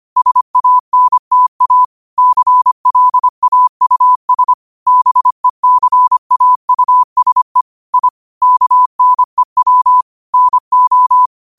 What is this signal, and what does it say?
Morse code, heard as a keyed tone, about 25 wpm, 1000 Hz. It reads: SANTA CLAUS BECAUSE I KNEW NO
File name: 25wpm